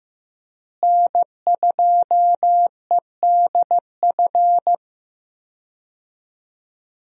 Morse code, heard as a keyed tone, about 15 wpm, 700 Hz. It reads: N2EDF
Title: USA Call Signs - 5 Characters - Repeated 3x - 15wpm